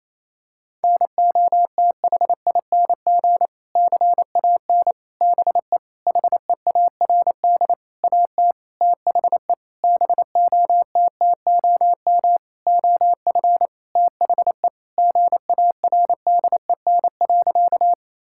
Morse code, heard as a keyed tone, about 27 wpm, 700 Hz. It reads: NOTHING CAN BE HEARD AT THE BOTTOM OF THE GARDEN.
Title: Sentences from Top 1000 Words - Review Part 6 - 28wpm